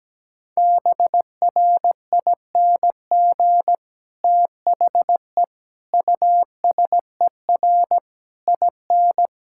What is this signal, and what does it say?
Morse code, heard as a keyed tone, about 17 wpm, 700 Hz. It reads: BRING THE USER IN
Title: Sentences from Top 1000 Words - Review Part 5 - 17wpm